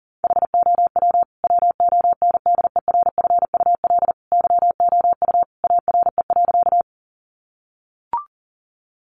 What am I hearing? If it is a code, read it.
Morse code, heard as a keyed tone, about 40 wpm, 700 Hz. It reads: HOW WONDERFUL YOU ARE.
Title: Sentences from Top 1100 Words 40wpm